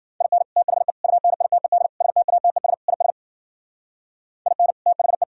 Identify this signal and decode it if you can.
Morse code, heard as a keyed tone, about 51 wpm, 700 Hz. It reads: IN THE UNITED STATES IS IN THE